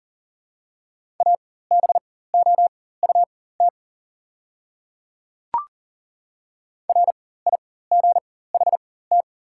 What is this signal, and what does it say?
Morse code, heard as a keyed tone, about 40 wpm, spaced as for 19 wpm, 700 Hz. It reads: ABOUT RIGHT
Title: 4x Character Spacing - Top 500 Words - 40wpm